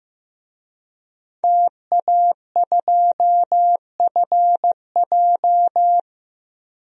Morse code, heard as a keyed tone, about 15 wpm, 700 Hz. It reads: TA2FJ
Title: DXCC-100 Call Signs - 5 Characters - 15wpm